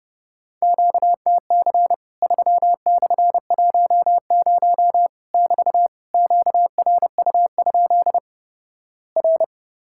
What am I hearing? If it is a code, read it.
Morse code, heard as a keyed tone, about 32 wpm, 700 Hz. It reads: QTC 3/10 <BT> QRU? R
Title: Worked All Europe DX Contest - QTC Exchange - 30wpm